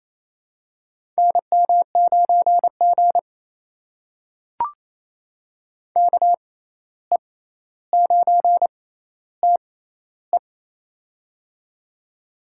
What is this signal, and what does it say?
Morse code, heard as a keyed tone, about 28 wpm, spaced as for 13 wpm, 700 Hz. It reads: NM9G KE9TE
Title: USA Call Signs - 6x Character Spacing - 28wpm